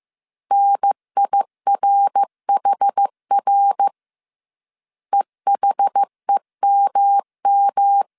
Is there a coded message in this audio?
Yes: Morse code from 0.5 to 8.0 s, heard as a keyed tone, about 15 wpm, 800 Hz. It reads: NIRHR EHEMM